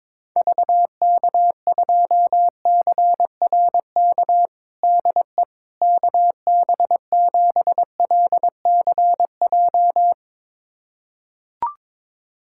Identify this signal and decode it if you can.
Morse code, heard as a keyed tone, about 22 wpm, 700 Hz. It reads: VK2CRK DE KB7LCJ
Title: USA Callsign DE Callsign 22wpm